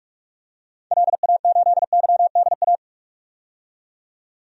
Morse code, heard as a keyed tone, about 42 wpm, 700 Hz. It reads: LA8YDA
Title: DXCC-100 Call Signs - 6 Characters - 45wpm